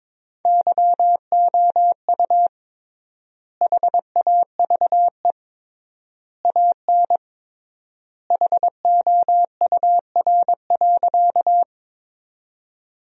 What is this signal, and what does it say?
Morse code, heard as a keyed tone, about 21 wpm, 700 Hz. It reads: YOU HAVE AN HOUR.
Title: 3x Word-Spacing - Sentences from Top 400 Words - 22wpm